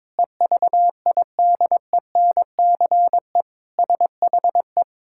Morse code, heard as a keyed tone, about 22 wpm, 700 Hz. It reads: EVIDENCE SHE
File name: practice / 22wpm / Sentences with 4-Word Phrases - 22wpm